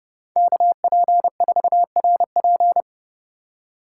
Morse code, heard as a keyed tone, about 29 wpm, 700 Hz. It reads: KP4RP